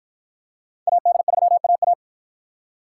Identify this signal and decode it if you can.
Morse code, heard as a keyed tone, about 53 wpm, 700 Hz. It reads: AD3AA